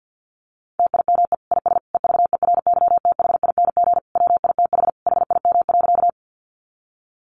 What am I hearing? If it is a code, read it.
Morse code, heard as a keyed tone, about 50 wpm, 700 Hz. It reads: TIME IS EVERYTHING WITH HIM.
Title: Sentences from Top 700 Words 50wpm